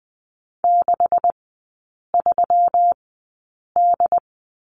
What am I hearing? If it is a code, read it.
Morse code, heard as a keyed tone, about 20 wpm, spaced as for 8 wpm, 700 Hz. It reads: 63D
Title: 127 Lesson 21e - Number 6 - Three Characters 20wpm